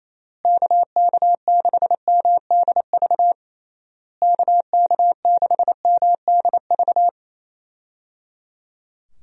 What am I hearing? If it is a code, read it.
Morse code, heard as a keyed tone, about 28 wpm, 700 Hz. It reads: KK6MDV KK6MDV
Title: USA Call Signs - 6 Characters - Repeated 3x - 28wpm